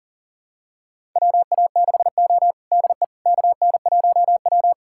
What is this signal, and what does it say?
Morse code, heard as a keyed tone, about 37 wpm, 700 Hz. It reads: WA6O DE KN1W
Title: USA Callsign DE Callsign - 4 Characters - 40wpm